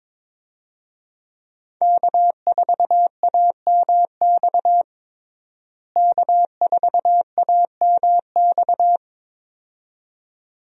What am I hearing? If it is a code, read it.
Morse code, heard as a keyed tone, about 22 wpm, 700 Hz. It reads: K4AMX K4AMX
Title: USA Call Signs - 5 Characters - Repeated 2x - 22wpm